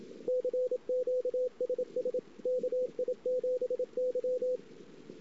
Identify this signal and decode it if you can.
Morse code, heard as a keyed tone, about 27 wpm, 500 Hz, with radio noise. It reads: CQ SS KI7Y
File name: ss-cw